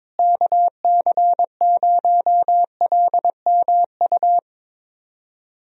Morse code, heard as a keyed tone, about 22 wpm, 700 Hz. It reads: KC0LMU
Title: USA Call Signs - 6 Characters - Repeated 3x - 22wpm